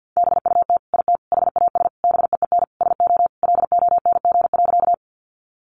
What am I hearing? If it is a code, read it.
Morse code, heard as a keyed tone, about 50 wpm, 700 Hz. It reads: BUT IT HAS BEEN SO LONG.
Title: Sentences from Top 100 Words 50wpm